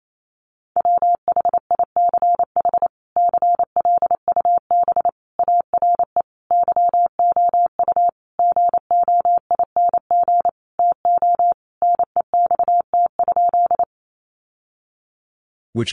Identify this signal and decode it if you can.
Morse code, heard as a keyed tone, about 28 wpm, 700 Hz. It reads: WHICH CLUB ARE YOU GOING TO NEXT?
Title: Sentences from Top 1200 Words 28wpm